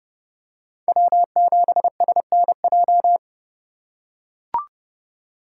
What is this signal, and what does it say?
Morse code, heard as a keyed tone, about 32 wpm, 700 Hz. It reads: W7SNJ
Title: USA Call Signs - 3x Character Spacing - 30wpm